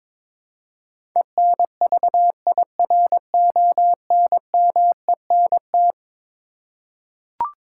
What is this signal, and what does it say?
Morse code, heard as a keyed tone, about 21 wpm, 700 Hz. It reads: ENVIRONMENT
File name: practice / 22wpm / Eleven Letter Words - 22wpm